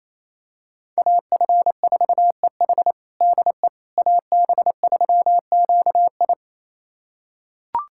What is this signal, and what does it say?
Morse code, heard as a keyed tone, about 28 wpm, 700 Hz. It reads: AF4EH DE AB3QI
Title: USA Callsign DE Callsign - 5 Characters - 28wpm